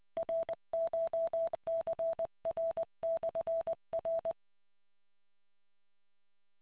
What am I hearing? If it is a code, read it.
Morse code, heard as a keyed tone, about 20 wpm, 660 Hz. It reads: RCHCR/R